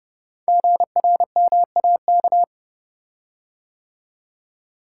Morse code, heard as a keyed tone, about 30 wpm, 700 Hz. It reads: GRMAK
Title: Code Groups - 5 Letters - 30wpm